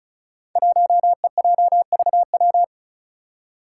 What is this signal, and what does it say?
Morse code, heard as a keyed tone, about 34 wpm, 700 Hz. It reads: 1EJVW
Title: Five Challenging Letter-Number Combinations 35wpm